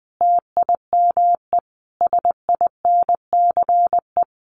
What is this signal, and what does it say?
Morse code, heard as a keyed tone, about 20 wpm, 700 Hz. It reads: TIME SINCE
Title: Sets of 5 Words Encore 20wpm